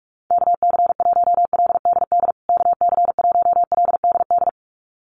Morse code, heard as a keyed tone, about 44 wpm, 700 Hz. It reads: KC1LDD KC1LDD
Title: USA Call Signs Repeated 2x 45wpm